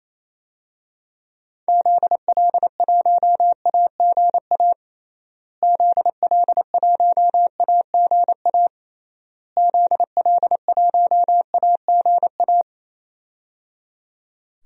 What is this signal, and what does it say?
Morse code, heard as a keyed tone, about 28 wpm, 700 Hz. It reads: ZL1AGA ZL1AGA ZL1AGA
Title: DXCC-100 Call Signs - 6 Characters - Repeated 3x - 28wpm